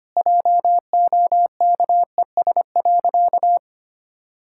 Morse code, heard as a keyed tone, about 25 wpm, 700 Hz. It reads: JOKES.
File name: practice / 25wpm / Sentences with 2-Word Phrases - 25wpm